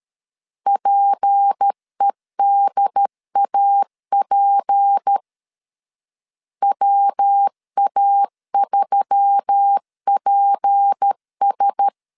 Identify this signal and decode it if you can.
Morse code, heard as a keyed tone, about 13 wpm, 800 Hz. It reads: PEDAP WA3PS
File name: LEZIONI / 6 PKX0 / 6 P 12wpm 2033